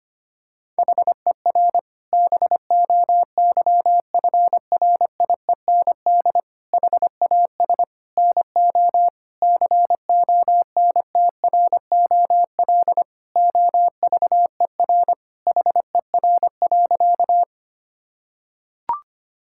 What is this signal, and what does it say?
Morse code, heard as a keyed tone, about 25 wpm, 700 Hz. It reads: HER BOYFRIEND HAS NO CONTROL OVER HER.
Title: Sentences from Top 1000 Words - Review Part 5 - 25wpm